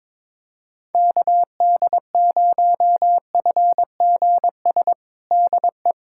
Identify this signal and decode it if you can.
Morse code, heard as a keyed tone, about 22 wpm, 700 Hz. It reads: KD0FGS DE
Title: USA Callsign DE Callsign - 6 Characters - 22wpm